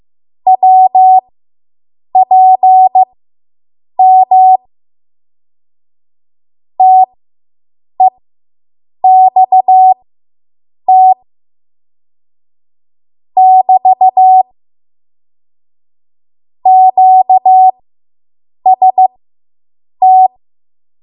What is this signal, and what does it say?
Morse code, heard as a keyed tone, about 15 wpm, spaced as for 7 wpm, 750 Hz. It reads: WPM TEXT <BT> QST